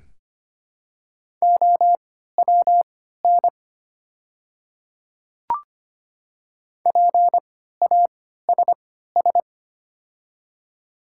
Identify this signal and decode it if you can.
Morse code, heard as a keyed tone, about 25 wpm, spaced as for 14 wpm, 700 Hz. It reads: OWN PASS
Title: 3x Character Spacing - Top 500 Words - 25wpm